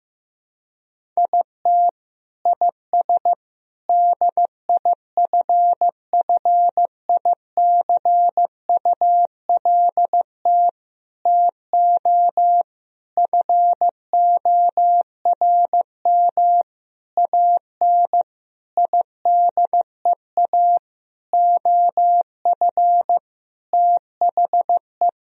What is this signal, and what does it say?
Morse code, heard as a keyed tone, about 15 wpm, 700 Hz. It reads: IT IS DIFFICULT TO FORM AN IDEA OF THE
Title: Sentences from Top 1000 Words - Review Part 1 - 15wpm